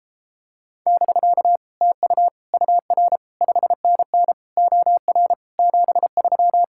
Morse code, heard as a keyed tone, about 29 wpm, 700 Hz. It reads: <BK> TU UR 5NN OR 73